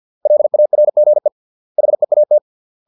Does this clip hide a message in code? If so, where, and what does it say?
Yes: Morse code from 0.3 to 2.4 s, heard as a keyed tone, about 50 wpm, 600 Hz. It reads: LARGE HEAT